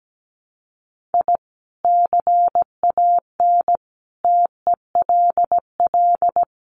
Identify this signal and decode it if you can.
Morse code, heard as a keyed tone, about 17 wpm, 700 Hz. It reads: I CAN TELL
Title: Sets of 4 Words 17wpm